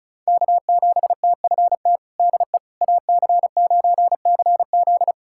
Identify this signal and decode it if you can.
Morse code, heard as a keyed tone, about 36 wpm, 700 Hz. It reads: K7TFT DE AC9CZ
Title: USA Callsign DE Callsign - 5 Characters - 35wpm